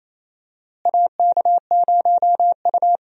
Morse code, heard as a keyed tone, about 27 wpm, 700 Hz. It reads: AK0U
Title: USA Call Signs - 4 Characters - Repeated 3x - 28wpm